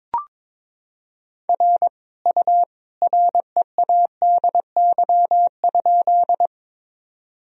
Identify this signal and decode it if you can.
Morse code, heard as a keyed tone, about 22 wpm, 700 Hz. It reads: R U READY?
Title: Sentences from Top 400 Words and Abbreviations 22wpm